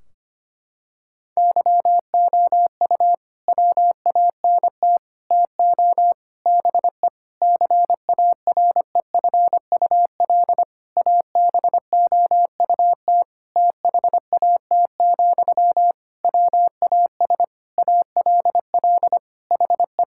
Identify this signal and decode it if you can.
Morse code, heard as a keyed tone, about 25 wpm, 700 Hz. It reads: YOU WANT TO BE CAREFUL ABOUT THAT, WAS ALL HE
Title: Sentences from Top 1600 Words 25wpm